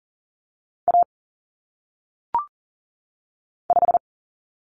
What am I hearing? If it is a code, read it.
Morse code, heard as a keyed tone, about 40 wpm, 700 Hz. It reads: A 5
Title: Common Cut Numbers - 1-Digit 40wpm